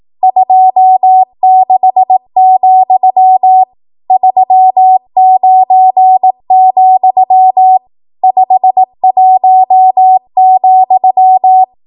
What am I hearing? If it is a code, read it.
Morse code, heard as a keyed tone, about 18 wpm, 750 Hz. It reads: 26, 39, 51,
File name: Archive / 18 WPM / 18WPM